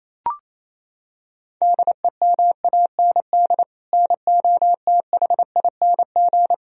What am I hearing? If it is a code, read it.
Morse code, heard as a keyed tone, about 28 wpm, 700 Hz. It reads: DEMAND NOTHING
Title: Sentences from Top 1000 Words - Review Part 4 - 28wpm